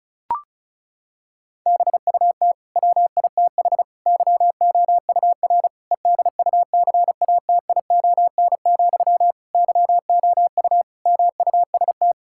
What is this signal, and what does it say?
Morse code, heard as a keyed tone, about 36 wpm, 700 Hz. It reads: BUT WITH YOUR EDUCATION, YOU MUST
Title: Sentences from Top 1000 Words - Review Part 6 - 35wpm